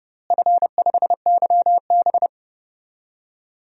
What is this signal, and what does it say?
Morse code, heard as a keyed tone, about 30 wpm, 700 Hz. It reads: F5YB
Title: DXCC-100 Call Signs - 4 Characters - Repeated 3x - 30wpm